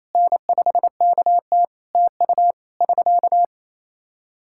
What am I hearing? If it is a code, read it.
Morse code, heard as a keyed tone, about 28 wpm, 700 Hz. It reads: N5KT TU <SK>
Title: Practice QSO - Exchange 4 - Ending - 28wpm